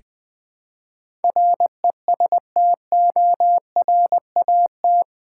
Morse code, heard as a keyed tone, about 20 wpm, 700 Hz. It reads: RESTORAT
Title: Build A Word - 4 Letters At A Time - 11-Letter Words - 20wpm